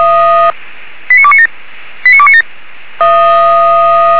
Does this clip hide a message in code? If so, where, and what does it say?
Yes: five-tone selective call (ZVEI) from 1.1 to 2.4 s, heard as quick runs of five tones, tones of about 70 ms. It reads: 89297 89297